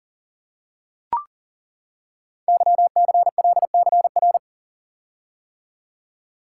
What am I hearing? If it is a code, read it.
Morse code, heard as a keyed tone, about 38 wpm, 690 Hz. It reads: YCLCR
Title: Code Groups - 5 Letters - 40wpm